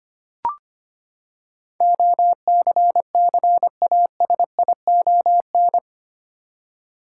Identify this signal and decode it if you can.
Morse code, heard as a keyed tone, about 25 wpm, 700 Hz. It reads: OCCASION
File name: Eight Letter Words - 25wpm